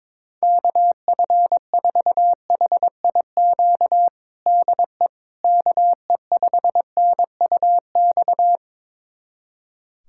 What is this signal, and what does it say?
Morse code, heard as a keyed tone, about 22 wpm, 700 Hz. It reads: KF4HIQ DE KE5NUX
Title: USA Callsign DE Callsign 22wpm